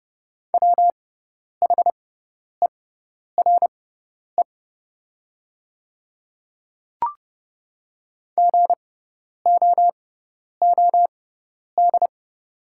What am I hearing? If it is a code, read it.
Morse code, heard as a keyed tone, about 30 wpm, spaced as for 10 wpm, 690 Hz. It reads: WHERE GOOD
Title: 6x Character Spacing - Top 200 Words - 30wpm